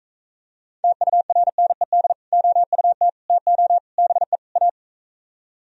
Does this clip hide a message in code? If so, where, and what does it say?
Yes: Morse code from 0.8 to 4.7 s, heard as a keyed tone, about 45 wpm, 700 Hz. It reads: TURNED OUT TO BE A